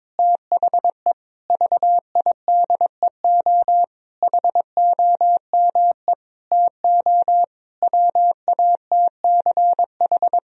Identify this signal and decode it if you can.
Morse code, heard as a keyed tone, about 22 wpm, 700 Hz. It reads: THE VIDEO HOME TO WATCH
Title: Sentences from Top 1000 Words - Review Part 4 - 22wpm